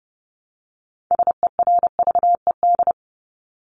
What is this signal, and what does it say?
Morse code, heard as a keyed tone, about 30 wpm, 700 Hz. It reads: SERVED
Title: ED Suffix - Top 100 Words with Base Words 30wpm